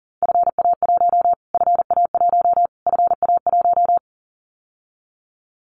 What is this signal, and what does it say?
Morse code, heard as a keyed tone, about 40 wpm, 700 Hz. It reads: FA1 FA1 FA1
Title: Callsign Prefixes for Top 25 DXCC - Repeated 3x - 40wpm